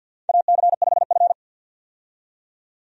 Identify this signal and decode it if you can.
Morse code, heard as a keyed tone, about 52 wpm, 700 Hz. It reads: AC5F
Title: USA Call Signs - 4 Characters - 50wpm